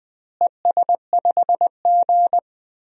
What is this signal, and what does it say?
Morse code, heard as a keyed tone, about 20 wpm, 700 Hz. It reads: ES5G
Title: DXCC-100 Call Signs - 4 Characters - Repeated 3x - 20wpm